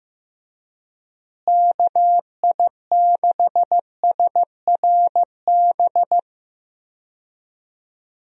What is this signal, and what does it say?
Morse code, heard as a keyed tone, about 15 wpm, 700 Hz. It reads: KI6SRB
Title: USA Call Signs - 6 Characters - Repeated 3x - 15wpm